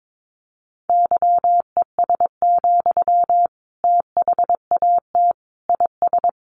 Morse code, heard as a keyed tone, about 22 wpm, 700 Hz. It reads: YES, THAT IS
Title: Sentences from Top 1500 Words 22wpm